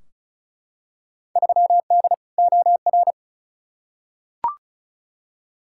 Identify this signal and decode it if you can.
Morse code, heard as a keyed tone, about 36 wpm, 700 Hz. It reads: 3D OR
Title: Field Day Exchanges - 35wpm